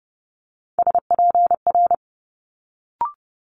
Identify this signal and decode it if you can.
Morse code, heard as a keyed tone, about 30 wpm, 700 Hz. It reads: SPR
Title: Multi-letter Phonemes Quick Fire 30wpm